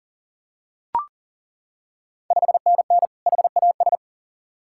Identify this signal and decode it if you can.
Morse code, heard as a keyed tone, about 40 wpm, 690 Hz. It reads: 5NN HAS
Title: Ontario QSO Party Exchange 40wpm